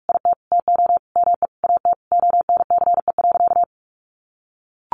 Morse code, heard as a keyed tone, about 44 wpm, 700 Hz. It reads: IT TO ME AT ONCE.
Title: Sentences from Top 500 Words 45wpm